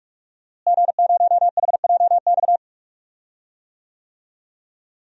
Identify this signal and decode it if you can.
Morse code, heard as a keyed tone, about 46 wpm, 700 Hz. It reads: G0HJX
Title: DXCC-100 Call Signs - 5 Characters - Repeated 2x - 45wpm